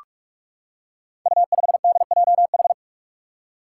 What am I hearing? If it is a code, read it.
Morse code, heard as a keyed tone, about 45 wpm, 700 Hz. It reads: U5DJH
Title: Five Challenging Letter-Number Combinations 45wpm